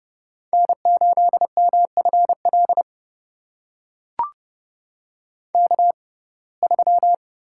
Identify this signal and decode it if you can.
Morse code, heard as a keyed tone, about 30 wpm, spaced as for 22 wpm, 700 Hz. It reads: N8MFL K3
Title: USA Call Signs - 5 Characters - 6x Character Spacing - 30wpm